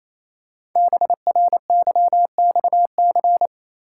Morse code, heard as a keyed tone, about 29 wpm, 700 Hz. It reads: BRYXC
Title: Code Groups - 5 Letters - 28wpm